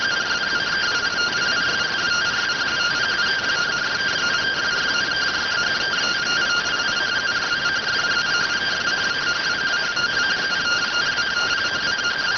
Two-tone teleprinter signal, about 170 Hz shift.